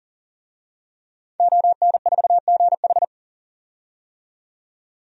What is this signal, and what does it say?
Morse code, heard as a keyed tone, about 40 wpm, 700 Hz. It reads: ON4GH
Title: DXCC-100 Call Signs - 5 Characters - 40wpm